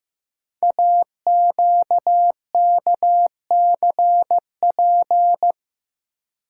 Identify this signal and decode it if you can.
Morse code, heard as a keyed tone, about 15 wpm, 700 Hz. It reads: AQKCP
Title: Code Groups - 5 Letters - 15wpm